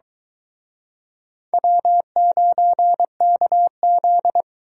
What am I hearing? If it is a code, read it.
Morse code, heard as a keyed tone, about 22 wpm, 700 Hz. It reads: W9KZ